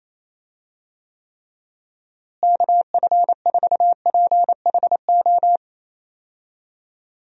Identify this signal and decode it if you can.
Morse code, heard as a keyed tone, about 28 wpm, 700 Hz. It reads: KF4PHO
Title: USA Call Signs - 6 Characters - Repeated 3x - 28wpm